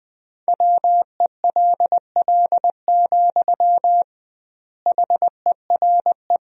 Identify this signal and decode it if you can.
Morse code, heard as a keyed tone, about 20 wpm, 700 Hz. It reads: WELL, HERE
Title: 2x Word-Spacing - Sentences from Top 300 Words - 20wpm